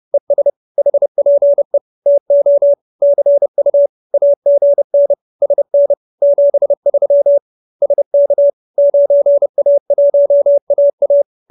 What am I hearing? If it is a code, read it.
Morse code, heard as a keyed tone, about 30 wpm, 570 Hz. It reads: ES HPE TO CU AGN SN 73 SK 9A1AA